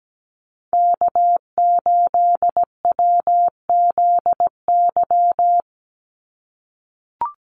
Morse code, heard as a keyed tone, about 17 wpm, 700 Hz. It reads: K8WZY
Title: USA Call Signs 17wpm